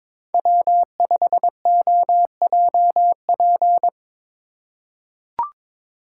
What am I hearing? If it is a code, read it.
Morse code, heard as a keyed tone, about 23 wpm, 700 Hz. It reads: W5OJP
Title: USA Call Signs - 5 Characters - 4x Character Spacing - 22wpm